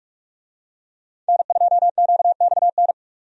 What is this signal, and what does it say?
Morse code, heard as a keyed tone, about 44 wpm, 700 Hz. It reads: N2QXN
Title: USA Call Signs - 5 Characters - 45wpm